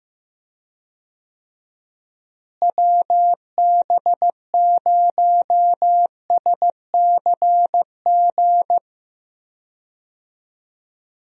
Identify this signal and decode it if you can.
Morse code, heard as a keyed tone, about 16 wpm, 700 Hz. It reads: WB0SCG
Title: USA Call Signs - 6 Characters - Repeated 2x - 15wpm